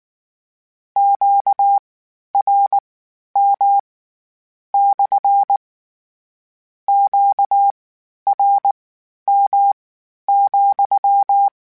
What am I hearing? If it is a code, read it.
Morse code, heard as a keyed tone, about 19 wpm, spaced as for 11 wpm, 800 Hz. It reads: QRM / QRM,
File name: Q Calls / Q Calls-mp3s / Q Calls SPSL2 @18